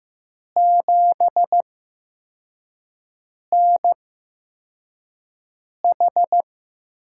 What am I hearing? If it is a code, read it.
Morse code, heard as a keyed tone, about 15 wpm, spaced as for 4 wpm, 700 Hz. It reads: 7NH
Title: USA Call Signs - 8x Character Spacing - 15wpm